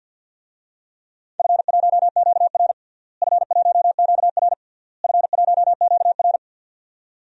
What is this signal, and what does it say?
Morse code, heard as a keyed tone, about 52 wpm, 700 Hz. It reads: F1QR F1QR F1QR
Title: DXCC-100 Call Signs - 4 Characters - Repeated 3x - 50wpm